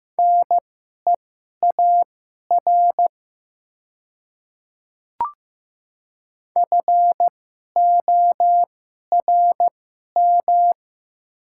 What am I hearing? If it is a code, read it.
Morse code, heard as a keyed tone, about 15 wpm, spaced as for 11 wpm, 700 Hz. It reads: NEAR FORM
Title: 2x Character Spacing - Top 200 Words - 15wpm